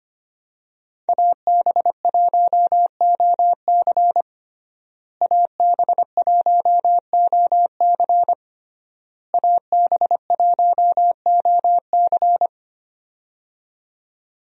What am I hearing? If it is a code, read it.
Morse code, heard as a keyed tone, about 25 wpm, 700 Hz. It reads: AB1OC AB1OC AB1OC
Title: USA Call Signs - 5 Characters - Repeated 3x - 25wpm